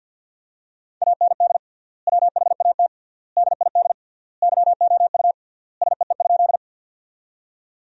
Morse code, heard as a keyed tone, about 53 wpm, 700 Hz. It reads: AND WHAT DID YOU SEE?
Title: 3x Word-Spacing - Sentences from Top 100 Words - 50wpm